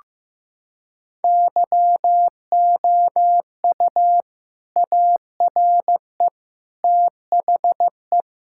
Morse code, heard as a keyed tone, about 16 wpm, 700 Hz. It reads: YOU ARE THE
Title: Sentences from Top 1000 Words - Review Part 5 - 15wpm